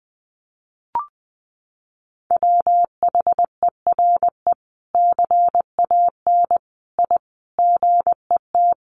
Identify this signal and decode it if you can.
Morse code, heard as a keyed tone, about 20 wpm, 700 Hz. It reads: WHERE CAN I GET
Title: Sentences from Top 1100 Words 20wpm